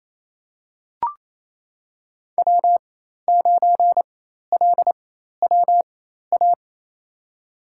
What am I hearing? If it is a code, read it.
Morse code, heard as a keyed tone, about 28 wpm, spaced as for 13 wpm, 700 Hz. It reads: W9LWA
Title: USA Call Signs - 4x Character Spacing - 28wpm